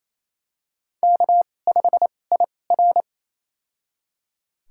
Morse code, heard as a keyed tone, about 28 wpm, spaced as for 20 wpm, 700 Hz. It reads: K5IR
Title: USA Call Signs - 4 Characters - 2x Character Spacing - 28wpm